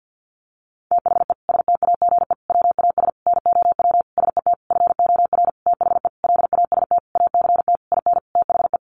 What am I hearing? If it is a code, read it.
Morse code, heard as a keyed tone, about 48 wpm, 700 Hz. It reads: THE STAGE WAS NOW SET FOR THE LAST ACT IN THE
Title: Sentences from Top 1500 Words 50wpm